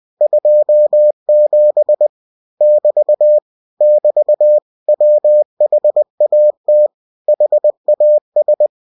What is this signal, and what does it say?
Morse code, heard as a keyed tone, about 20 wpm, 600 Hz. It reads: 27 <BT> <BT> WHAT HAS
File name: cw-books / winnie-the-pooh / winnie-20wpm-page-0027